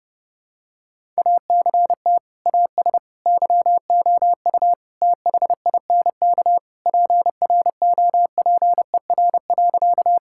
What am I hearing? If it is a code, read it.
Morse code, heard as a keyed tone, about 30 wpm, 700 Hz. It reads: ACT AS YOU THINK PROPER.
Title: Sentences from Top 1000 Words - Review Part 5 - 30wpm